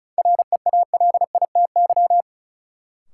Morse code, heard as a keyed tone, about 36 wpm, 700 Hz. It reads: REALITY
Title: Sentences from Top 1200 Words - New Word Drill Alt - 35wpm